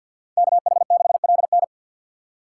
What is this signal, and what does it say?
Morse code, heard as a keyed tone, about 64 wpm, 700 Hz. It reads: KH6LN